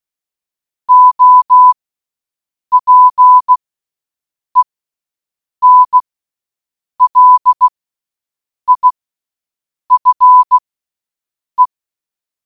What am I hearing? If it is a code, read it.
Morse code, heard as a keyed tone, about 16 wpm, spaced as for 7 wpm, 1000 Hz. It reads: OPEN LIFE